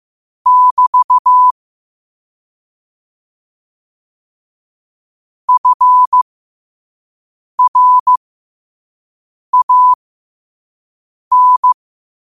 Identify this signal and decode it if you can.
Morse code, heard as a keyed tone, about 15 wpm, spaced as for 5 wpm, 1000 Hz. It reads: <BT> FRAN
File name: qotdfiles / 5wpm